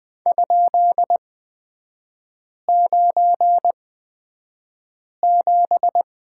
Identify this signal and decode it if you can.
Morse code, heard as a keyed tone, about 20 wpm, spaced as for 5 wpm, 700 Hz. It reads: ?97
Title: 243 Lesson 39e - Letter z - Three Characters 20wpm